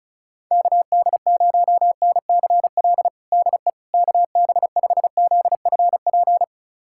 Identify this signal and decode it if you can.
Morse code, heard as a keyed tone, about 36 wpm, 700 Hz. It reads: KD0NCL DE KB5ZFP
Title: USA Callsign DE Callsign - 6 Characters - 35wpm